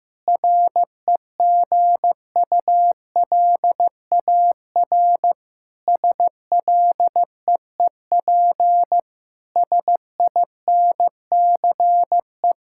Morse code, heard as a keyed tone, about 15 wpm, 700 Hz. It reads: REGULAR SLEEP SINCE
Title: Sentences from Top 1000 Words - Review Part 6 - 15wpm